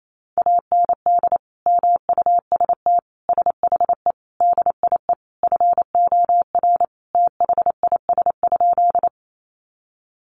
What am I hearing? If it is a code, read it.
Morse code, heard as a keyed tone, about 28 wpm, 700 Hz. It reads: AND MUST SHE DIE FOR THIS?
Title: Sentences from Top 1300 Words 28wpm